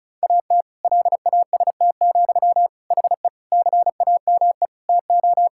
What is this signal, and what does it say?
Morse code, heard as a keyed tone, about 34 wpm, 700 Hz. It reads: AT LAST, HE CAME TO
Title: Sentences from Top 1600 Words 35wpm